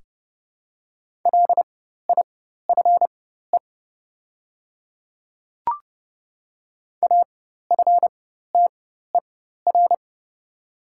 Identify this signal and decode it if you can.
Morse code, heard as a keyed tone, about 30 wpm, spaced as for 14 wpm, 690 Hz. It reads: LIFE AFTER
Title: 4x Character Spacing - Top 300 Words - 30wpm